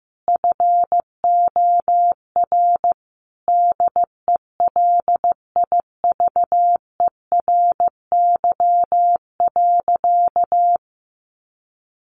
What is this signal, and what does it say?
Morse code, heard as a keyed tone, about 15 wpm, 700 Hz. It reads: FOR DELIVERY.
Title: Sentences from Top 1500 Words 15wpm